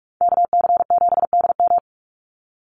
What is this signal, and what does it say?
Morse code, heard as a keyed tone, about 44 wpm, 700 Hz. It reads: KC7DM
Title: NA SOTA Activator Callsigns 45wpm